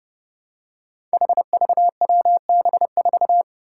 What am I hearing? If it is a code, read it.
Morse code, heard as a keyed tone, about 32 wpm, 700 Hz. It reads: HVWB4
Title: Five Challenging Letter-Number Combinations 30wpm